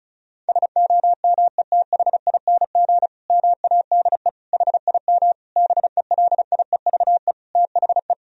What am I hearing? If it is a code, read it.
Morse code, heard as a keyed tone, about 34 wpm, 700 Hz. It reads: SOMETHING MADE HIM BELIEVE THE
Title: Sentences from Top 1000 Words - Review Part 1 - 35wpm